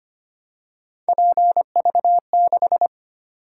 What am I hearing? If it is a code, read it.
Morse code, heard as a keyed tone, about 25 wpm, 700 Hz. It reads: PV6
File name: Three Letter-Number Combinations - 25wpm